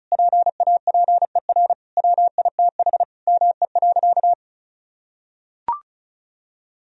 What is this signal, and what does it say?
Morse code, heard as a keyed tone, about 36 wpm, 700 Hz. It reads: PAPER WITH ME.
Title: Sentences from Top 1000 Words - Review Part 2 - 35wpm